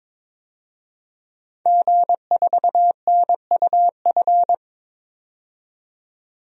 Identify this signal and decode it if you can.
Morse code, heard as a keyed tone, about 22 wpm, 700 Hz. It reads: G4NUF
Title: DXCC-100 Call Signs - 5 Characters - 22wpm